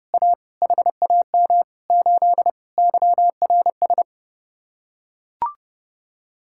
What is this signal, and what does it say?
Morse code, heard as a keyed tone, about 32 wpm, 700 Hz. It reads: A HAM 8 YRS